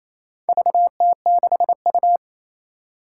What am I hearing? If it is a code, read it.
Morse code, heard as a keyed tone, about 29 wpm, 700 Hz. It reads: VT6U
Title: Four Challenging Letter-Number Combinations 28wpm